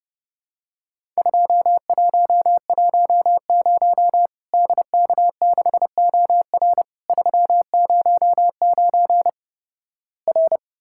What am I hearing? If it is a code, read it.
Morse code, heard as a keyed tone, about 30 wpm, 700 Hz. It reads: 2110 DK6OR 309 R
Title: Worked All Europe DX Contest - QTC Exchange - 30wpm